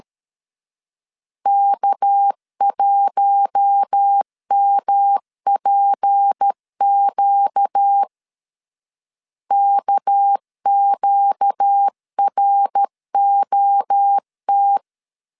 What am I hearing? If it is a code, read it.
Morse code, heard as a keyed tone, about 13 wpm, 800 Hz. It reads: K1MPQ KQROT
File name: LEZIONI / 6 PKX0 / 6 PK 12wpm 2006